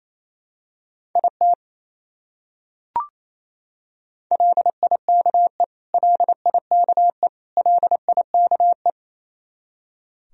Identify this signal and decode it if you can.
Morse code, heard as a keyed tone, about 28 wpm, 700 Hz. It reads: IT LIKE LIKE LIKE
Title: Top 100 Words Repeated 28wpm